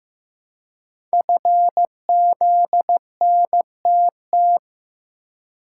Morse code, heard as a keyed tone, about 15 wpm, 700 Hz. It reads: FZNTT
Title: Code Groups - 5 Letters - 15wpm